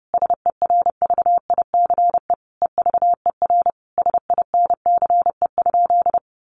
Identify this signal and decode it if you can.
Morse code, heard as a keyed tone, about 30 wpm, 700 Hz. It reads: SERVICE EVER SINCE?
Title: Sentences from Top 700 Words 30wpm